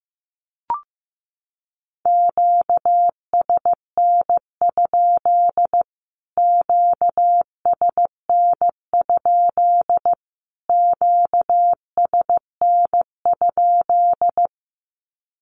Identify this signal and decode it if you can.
Morse code, heard as a keyed tone, about 15 wpm, 700 Hz. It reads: QSN? QSN? QSN?
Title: Q-Codes Repeated 3x 15wpm